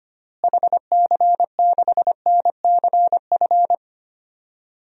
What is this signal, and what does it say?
Morse code, heard as a keyed tone, about 25 wpm, 700 Hz. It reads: HC6NCF
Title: DXCC-100 Call Signs - 6 Characters - Repeated 3x - 25wpm